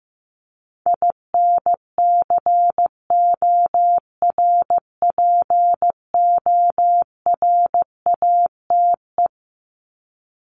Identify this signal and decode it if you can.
Morse code, heard as a keyed tone, about 15 wpm, 700 Hz. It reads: INCORPORATE
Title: Top 1751-2000 Words 15wpm